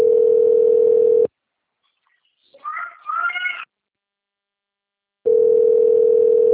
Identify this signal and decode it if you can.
Ringback tone (North America), 440 + 480 Hz.